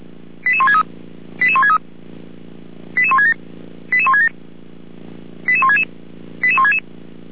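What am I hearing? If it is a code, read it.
Five-tone selective call (ZVEI), heard as quick runs of five tones, tones of about 70 ms. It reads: 88163 88163 88167 88167 88166 88166
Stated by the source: Five-tone ZVEI selective call received on SDR, fire brigade feuerwehr-orlamuende.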